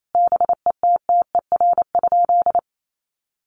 Morse code, heard as a keyed tone, about 28 wpm, 700 Hz. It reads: BETTER?
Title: Sentences from Top 1500 Words 28wpm